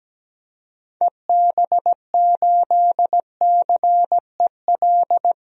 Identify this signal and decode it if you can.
Morse code, heard as a keyed tone, about 17 wpm, 700 Hz. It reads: EB8CEL
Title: DXCC-100 Call Signs - 6 Characters - 17wpm